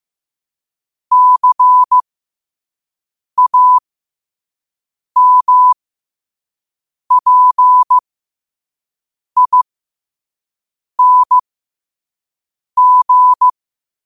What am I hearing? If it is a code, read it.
Morse code, heard as a keyed tone, about 15 wpm, spaced as for 5 wpm, 1000 Hz. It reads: CAMPING
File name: qotdfiles / 5wpm